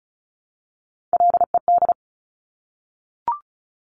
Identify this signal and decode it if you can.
Morse code, heard as a keyed tone, about 36 wpm, 700 Hz. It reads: LED
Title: Top Three Letter Combinations in Words 35wpm